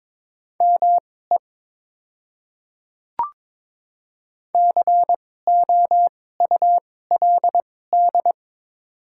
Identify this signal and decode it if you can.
Morse code, heard as a keyed tone, about 22 wpm, spaced as for 16 wpm, 700 Hz. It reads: ME COULD
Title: 2x Character Spacing - Top 200 Words - 22wpm